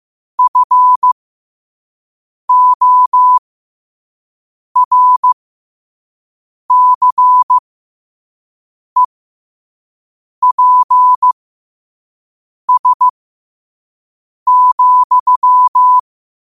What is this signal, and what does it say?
Morse code, heard as a keyed tone, about 15 wpm, spaced as for 5 wpm, 1000 Hz. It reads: FORCEPS,
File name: qotdfiles / 5wpm